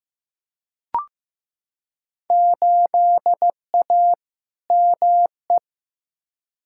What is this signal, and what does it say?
Morse code, heard as a keyed tone, about 15 wpm, 700 Hz. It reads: 8A ME
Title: Field Day Exchanges - 15wpm